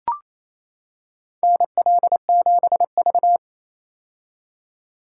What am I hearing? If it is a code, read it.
Morse code, heard as a keyed tone, about 28 wpm, 700 Hz. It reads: NL7V
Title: K1USN SST QSO Practice - 28wpm